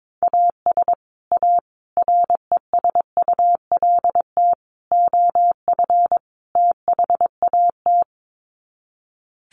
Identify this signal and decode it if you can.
Morse code, heard as a keyed tone, about 22 wpm, 700 Hz. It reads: AS A RESULT OF THAT
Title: Sets of 5 Words Encore 22wpm